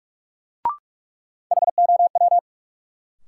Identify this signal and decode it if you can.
Morse code, heard as a keyed tone, about 46 wpm, 700 Hz. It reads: HOW
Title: Sentences from Top 100 Words - New Word Drill - 45wpm